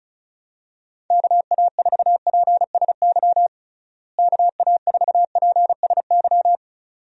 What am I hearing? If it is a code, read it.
Morse code, heard as a keyed tone, about 36 wpm, 700 Hz. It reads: KA4PSY KA4PSY
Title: USA Call Signs - 6 Characters - Repeated 2x - 35wpm